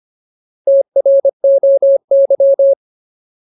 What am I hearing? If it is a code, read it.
Morse code, heard as a keyed tone, about 25 wpm, 550 Hz. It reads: TROY
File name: Ultimate Word List / 25WPM Ultimate